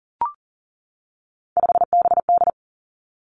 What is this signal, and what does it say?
Morse code, heard as a keyed tone, about 42 wpm, 700 Hz. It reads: 5BD
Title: Three Challenging Letter-Number Combinations 40wpm